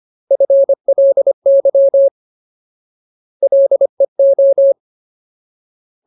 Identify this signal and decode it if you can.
Morse code, heard as a keyed tone, about 25 wpm, 550 Hz. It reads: FLY LEO
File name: Ultimate Word List / 25WPM Ultimate